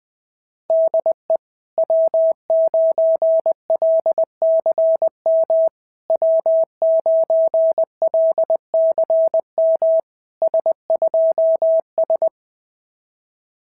Morse code, heard as a keyed tone, about 20 wpm, 650 Hz. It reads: DE W9LCM W9LCM S2S
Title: SOTA QSO Practice 20wpm